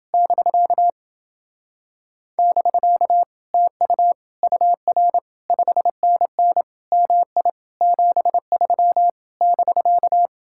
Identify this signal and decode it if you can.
Morse code, heard as a keyed tone, about 29 wpm, 700 Hz. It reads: <BK> <BK> TU UR 5NN MI 73 <BK>